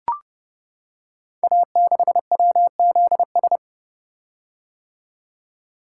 Morse code, heard as a keyed tone, about 30 wpm, 700 Hz. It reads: A6WZS
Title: Code Groups - 5 Letter:Numbers - 30wpm